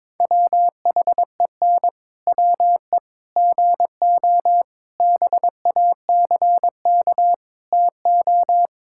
Morse code, heard as a keyed tone, about 21 wpm, 700 Hz. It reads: WHEN WE GO BACK TO
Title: Pangram Challenge v2 - 22wpm